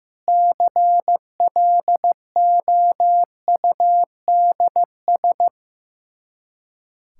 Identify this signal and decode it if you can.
Morse code, heard as a keyed tone, about 15 wpm, 700 Hz. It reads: CLOUDS
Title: K1USN SST QSO Practice 15wpm